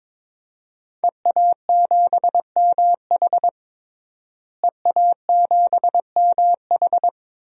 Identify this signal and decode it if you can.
Morse code, heard as a keyed tone, about 22 wpm, 700 Hz. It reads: EA7MH EA7MH
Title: DXCC-100 Call Signs - 5 Characters - Repeated 2x - 22wpm